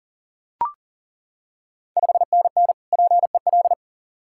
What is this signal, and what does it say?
Morse code, heard as a keyed tone, about 38 wpm, 690 Hz. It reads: 5NN PEL
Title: Ontario QSO Party Exchange 40wpm